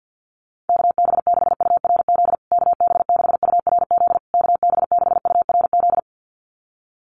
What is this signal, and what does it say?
Morse code, heard as a keyed tone, about 48 wpm, 700 Hz. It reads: KB6URZ KB6URZ KB6URZ
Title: USA Call Signs Repeated 3x 50wpm